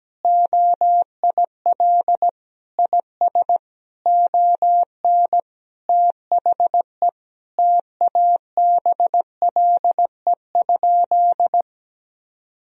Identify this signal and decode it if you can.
Morse code, heard as a keyed tone, about 17 wpm, 700 Hz. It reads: OIL IS ON THE TABLE?
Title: Sentences from Top 1000 Words - Review Part 4 - 17wpm